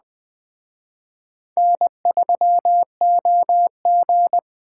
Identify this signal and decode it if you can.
Morse code, heard as a keyed tone, about 21 wpm, 700 Hz. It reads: N3OG